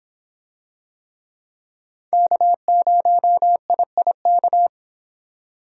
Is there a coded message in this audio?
Yes: Morse code from 2.1 to 4.7 s, heard as a keyed tone, about 25 wpm, 700 Hz. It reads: K0IIK